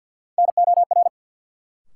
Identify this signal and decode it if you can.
Morse code, heard as a keyed tone, about 50 wpm, 700 Hz. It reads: NOR
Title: Sentences from Top 1500 Words - New Word Drill - 50wpm